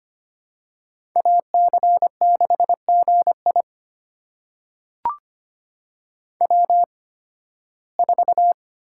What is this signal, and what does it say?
Morse code, heard as a keyed tone, about 25 wpm, spaced as for 16 wpm, 700 Hz. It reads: AC6GI W4
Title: USA Call Signs - 5 Characters - 8x Character Spacing - 25wpm